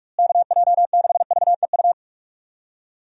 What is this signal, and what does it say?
Morse code, heard as a keyed tone, about 45 wpm, 700 Hz. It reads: KJ6VEU
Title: USA Call Signs - 6 Characters - Repeated 3x - 45wpm